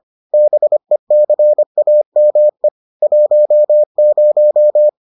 Morse code, heard as a keyed tone, about 25 wpm, 600 Hz. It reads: BECAME 10